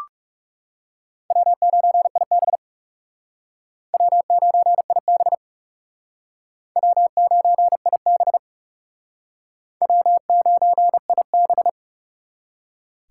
Morse code, heard as a keyed tone, about 36 wpm, 700 Hz. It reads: W9IB W9IB W9IB W9IB
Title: Speed Racing - USA 1x2 Call Signs 30wpm